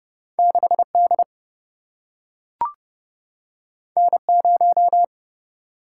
Morse code, heard as a keyed tone, about 30 wpm, 700 Hz. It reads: 6 D N 0
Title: Single-Letter Single-Number 30wpm